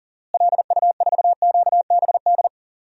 Morse code, heard as a keyed tone, about 38 wpm, 700 Hz. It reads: LU4QBD
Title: DXCC-100 Call Signs - 6 Characters - Repeated 3x - 40wpm